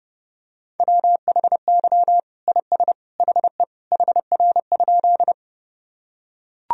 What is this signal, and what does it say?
Morse code, heard as a keyed tone, about 30 wpm, 700 Hz. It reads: WHY IS HE HR?
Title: Sentences from Top 200 Words and Abbreviations 30wpm